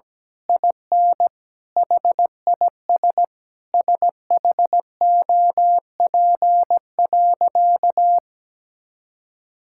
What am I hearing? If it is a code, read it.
Morse code, heard as a keyed tone, about 17 wpm, 700 Hz. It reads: IN HIS SHOP.
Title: Sentences from Top 1600 Words 17wpm